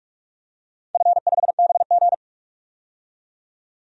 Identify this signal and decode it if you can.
Morse code, heard as a keyed tone, about 46 wpm, 700 Hz. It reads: F5BG
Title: DXCC-100 Call Signs - 4 Characters - 45wpm